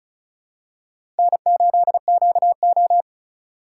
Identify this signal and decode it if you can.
Morse code, heard as a keyed tone, about 36 wpm, 700 Hz. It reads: N8QO
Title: USA Call Signs - 4 Characters - 3x Character Spacing - 35wpm